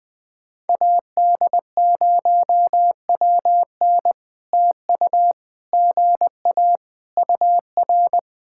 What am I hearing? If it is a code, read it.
Morse code, heard as a keyed tone, about 20 wpm, 700 Hz. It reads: AD0WN TU GA UR
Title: POTA QSO Practice - 20wpm